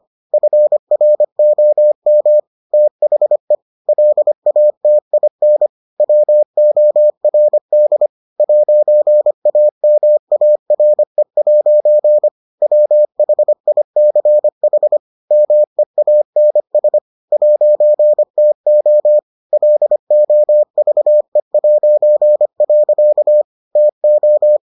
Morse code, heard as a keyed tone, about 25 wpm, 600 Hz. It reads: FROM THE LATIN WORD 'AMARE' WHICH MEANS 'TO LOVE'. TO